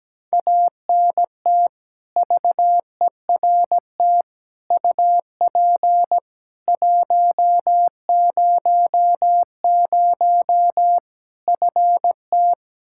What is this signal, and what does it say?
Morse code, heard as a keyed tone, about 17 wpm, 700 Hz. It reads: ANT VERT UP 100 FT
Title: Practice QSO - Exchange 2 - Info - 17wpm